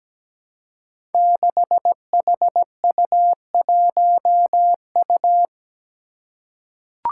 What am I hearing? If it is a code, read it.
Morse code, heard as a keyed tone, about 17 wpm, 700 Hz. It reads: 6HU1U
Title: Five Challenging Letter-Number Combinations 17wpm